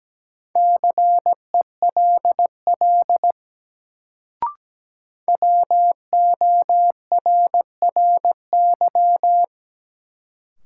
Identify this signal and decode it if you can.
Morse code, heard as a keyed tone, about 17 wpm, 700 Hz. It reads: CELL WORRY
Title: Sentences from Top 1100 Words - New Word Drill Alt - 17wpm